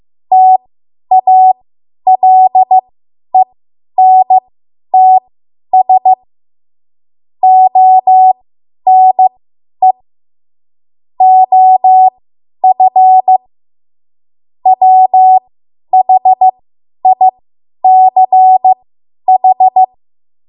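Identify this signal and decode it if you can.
Morse code, heard as a keyed tone, about 15 wpm, spaced as for 10 wpm, 750 Hz. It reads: TALENTS ONE OF WHICH
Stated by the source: Morse code practice run at 10 WPM from a code practice archive.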